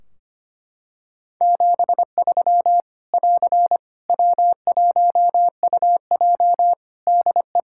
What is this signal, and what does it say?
Morse code, heard as a keyed tone, about 25 wpm, 700 Hz. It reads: 73 <AR> W1UJ DE
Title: Practice QSO - Exchange 4 - Ending - 25wpm